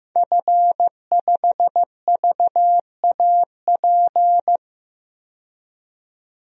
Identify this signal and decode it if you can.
Morse code, heard as a keyed tone, about 15 wpm, 700 Hz. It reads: F5VAP
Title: DXCC-100 Call Signs - 5 Characters - 15wpm